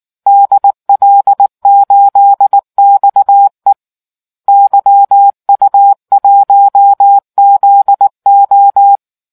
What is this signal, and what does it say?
Morse code, heard as a keyed tone, about 19 wpm, 800 Hz. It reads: DL8XE YU1ZO